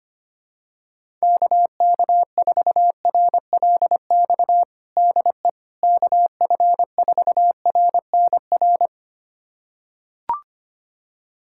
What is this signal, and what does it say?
Morse code, heard as a keyed tone, about 25 wpm, 700 Hz. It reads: KK4RLX DE KF4RNR
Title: USA Callsign DE Callsign - 6 Characters - 25wpm